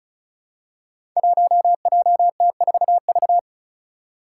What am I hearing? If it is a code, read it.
Morse code, heard as a keyed tone, about 36 wpm, 700 Hz. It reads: 1JT4V
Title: Five Challenging Letter-Number Combinations 35wpm